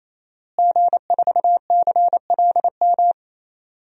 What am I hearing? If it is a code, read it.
Morse code, heard as a keyed tone, about 28 wpm, 700 Hz. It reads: G4CLM
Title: DXCC-100 Call Signs - 5 Characters - Repeated 3x - 28wpm